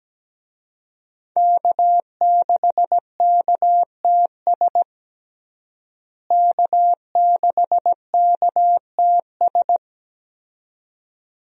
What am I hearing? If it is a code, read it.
Morse code, heard as a keyed tone, about 17 wpm, 700 Hz. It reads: K6KTS K6KTS
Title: USA Call Signs - 5 Characters - Repeated 2x - 17wpm